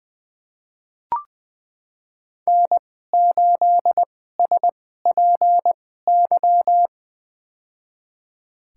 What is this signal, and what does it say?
Morse code, heard as a keyed tone, about 20 wpm, spaced as for 14 wpm, 700 Hz. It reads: N8SPY
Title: USA Call Signs - 5 Characters - 2x Character Spacing - 20wpm